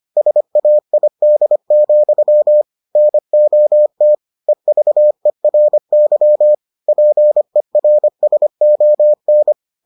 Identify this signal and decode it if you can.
Morse code, heard as a keyed tone, about 25 wpm, 600 Hz. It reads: SAID, NOT EVERY PERSON